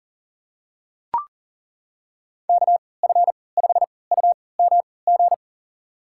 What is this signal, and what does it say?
Morse code, heard as a keyed tone, about 40 wpm, spaced as for 23 wpm, 700 Hz. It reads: KF5UMG
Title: USA Call Signs - 6 Characters - 3x Character Spacing - 40wpm